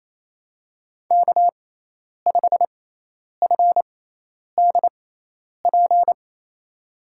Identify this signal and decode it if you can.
Morse code, heard as a keyed tone, about 27 wpm, spaced as for 10 wpm, 700 Hz. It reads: K5FDP
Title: USA Call Signs - 6x Character Spacing - 28wpm